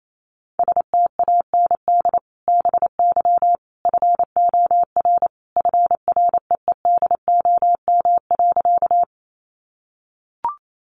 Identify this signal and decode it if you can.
Morse code, heard as a keyed tone, about 28 wpm, 700 Hz. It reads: STAND BY FOR FREEDOM.
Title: Sentences from Top 800 Words 28wpm